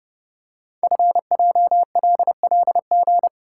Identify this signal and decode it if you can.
Morse code, heard as a keyed tone, about 30 wpm, 700 Hz. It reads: FJLLG
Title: Code Groups - 5 Letters - 30wpm